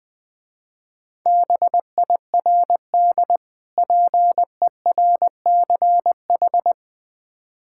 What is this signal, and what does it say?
Morse code, heard as a keyed tone, about 20 wpm, 700 Hz. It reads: BIRD PERCH
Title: Difficult Words with Hints - 20wpm